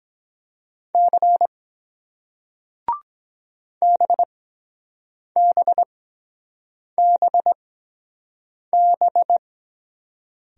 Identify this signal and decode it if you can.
Morse code, heard as a keyed tone, about 23 wpm, spaced as for 7 wpm, 700 Hz. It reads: C BBBB